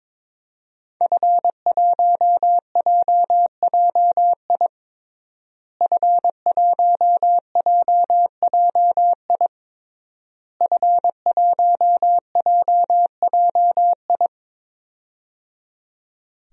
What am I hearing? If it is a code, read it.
Morse code, heard as a keyed tone, about 22 wpm, 700 Hz. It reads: F1JJI F1JJI F1JJI
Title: DXCC-100 Call Signs - 5 Characters - Repeated 3x - 22wpm